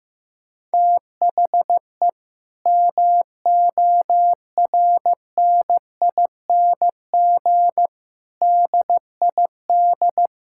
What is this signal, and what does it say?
Morse code, heard as a keyed tone, about 15 wpm, 700 Hz. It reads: THE MORNING DID
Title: Sentences from Top 1600 Words 15wpm